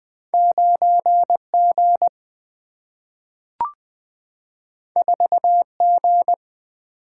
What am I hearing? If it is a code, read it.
Morse code, heard as a keyed tone, about 20 wpm, 700 Hz. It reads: 9G 4G
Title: Single-Letter Single-Number 20wpm